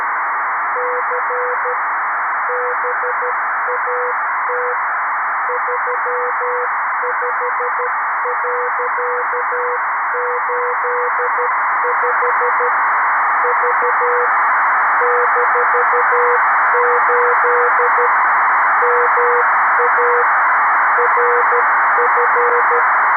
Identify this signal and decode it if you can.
Morse code, heard as a keyed tone, about 12 wpm, 480 Hz, with noise. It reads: C BAT 35.85 V -8 MA RF
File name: arissat / recordings / ARISSat 11Sept2011 1354utc 60frames